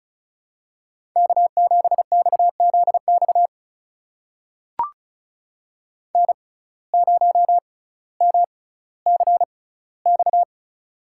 Morse code, heard as a keyed tone, about 34 wpm, spaced as for 18 wpm, 700 Hz. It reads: K7XZX N0MCX
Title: USA Call Signs - 5 Characters - 6x Character Spacing - 35wpm